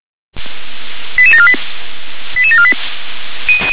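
Five-tone selective call (ZVEI), heard as quick runs of five tones, tones of about 70 ms. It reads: 88748 88748
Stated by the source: ZVEI five-tone selective call received by SDR.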